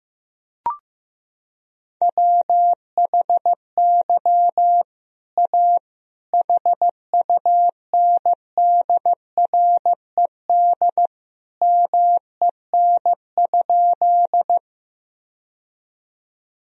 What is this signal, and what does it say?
Morse code, heard as a keyed tone, about 15 wpm, 700 Hz. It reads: WHY A HUNDRED MEN?
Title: Sentences from Top 1000 Words - Review Part 2 - 15wpm